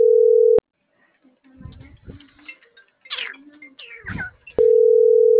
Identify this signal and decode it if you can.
Ringback tone (North America), 440 + 480 Hz.
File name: CENTRAL MAYAB / out michelle.martinez